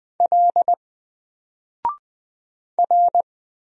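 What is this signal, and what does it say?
Morse code, heard as a keyed tone, about 20 wpm, 700 Hz. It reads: L R
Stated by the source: Single letter and number Morse drill at 20 wpm, from N4LCD.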